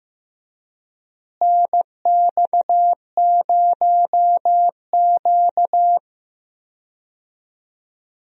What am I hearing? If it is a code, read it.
Morse code, heard as a keyed tone, about 15 wpm, 700 Hz. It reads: NX0Q
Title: USA Call Signs - 4 Characters - Repeated 3x - 15wpm